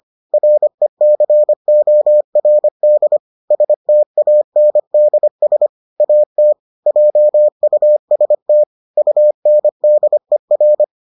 Morse code, heard as a keyed tone, about 25 wpm, 600 Hz. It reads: RECORD STANDS AT JUST UNDER